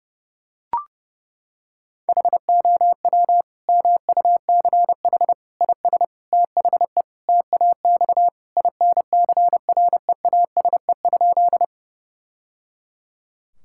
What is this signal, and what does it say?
Morse code, heard as a keyed tone, about 30 wpm, 700 Hz. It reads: HOW MUCH IS THE TAX INCREASE?
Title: Sentences from Top 1000 Words - Review Part 5 - 30wpm